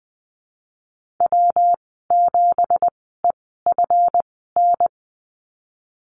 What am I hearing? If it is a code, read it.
Morse code, heard as a keyed tone, about 20 wpm, spaced as for 15 wpm, 700 Hz. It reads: W7EFN
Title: 200 Lesson 32g - Letter k - Call Signs 20wpm